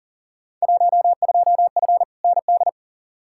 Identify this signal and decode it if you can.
Morse code, heard as a keyed tone, about 40 wpm, 700 Hz. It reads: 12F ND
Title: Field Day Exchanges Encore - 40wpm